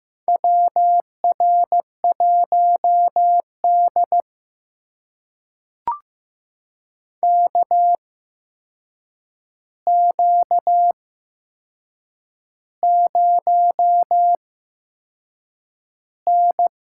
Morse code, heard as a keyed tone, about 15 wpm, spaced as for 6 wpm, 700 Hz. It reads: WR1D KQ0N
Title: USA Call Signs - 4 Characters - 8x Character Spacing - 15wpm